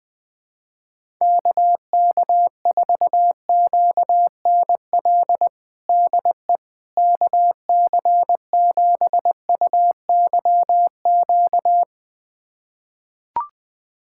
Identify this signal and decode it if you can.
Morse code, heard as a keyed tone, about 21 wpm, 700 Hz. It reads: KK4QNL DE KC7UYQ
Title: USA Callsign DE Callsign - 6 Characters - 20wpm